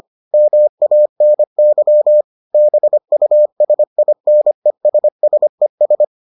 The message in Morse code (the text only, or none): MANY BUSINESSES